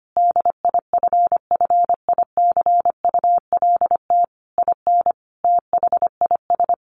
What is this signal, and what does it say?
Morse code, heard as a keyed tone, about 25 wpm, 700 Hz. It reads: DIFFICULT IN THIS
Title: Sentences from Top 1500 Words 25wpm